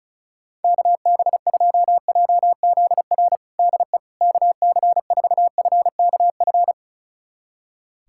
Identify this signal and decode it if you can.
Morse code, heard as a keyed tone, about 34 wpm, 700 Hz. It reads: KB2JZR DE KC4FKF
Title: USA Callsign DE Callsign - 6 Characters - 35wpm